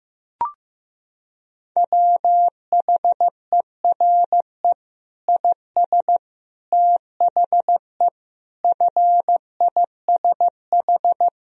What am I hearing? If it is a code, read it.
Morse code, heard as a keyed tone, about 15 wpm, 700 Hz. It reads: WHERE IS THE FISH
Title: Sentences from Top 1000 Words - Review Part 3 - 15wpm